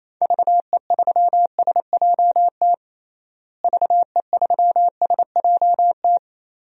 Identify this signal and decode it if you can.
Morse code, heard as a keyed tone, about 27 wpm, 700 Hz. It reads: VE3SJT VE3SJT
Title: DXCC-100 Call Signs - 6 Characters - Repeated 3x - 28wpm